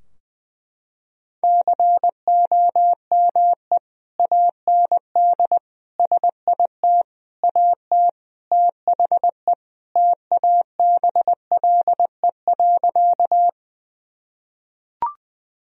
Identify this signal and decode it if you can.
Morse code, heard as a keyed tone, about 21 wpm, 700 Hz. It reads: COME AND SIT AT THE TABLE.
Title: Sentences from Top 1000 Words - Review Part 2 - 20wpm